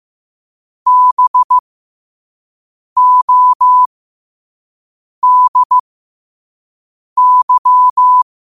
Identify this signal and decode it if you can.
Morse code, heard as a keyed tone, about 15 wpm, spaced as for 5 wpm, 1000 Hz. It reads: BODY